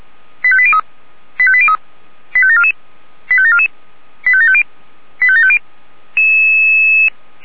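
Five-tone selective call (ZVEI), heard as quick runs of five tones, tones of about 70 ms. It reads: 85803 85803 85744 85744 85750 85750
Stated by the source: Five-tone ZVEI selective call received by SDR.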